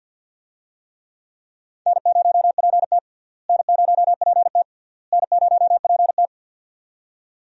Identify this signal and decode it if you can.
Morse code, heard as a keyed tone, about 49 wpm, 700 Hz. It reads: N0PT N0PT N0PT
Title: USA Call Signs - 4 Characters - Repeated 3x - 50wpm